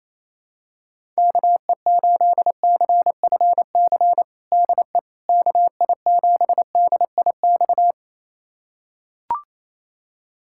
Morse code, heard as a keyed tone, about 27 wpm, 700 Hz. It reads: KE8CFC DE KI7DIX
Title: USA Callsign DE Callsign - 6 Characters - 28wpm